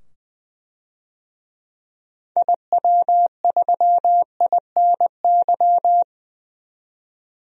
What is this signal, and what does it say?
Morse code, heard as a keyed tone, about 21 wpm, 700 Hz. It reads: IW3INY
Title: DXCC-100 Call Signs - 6 Characters - 20wpm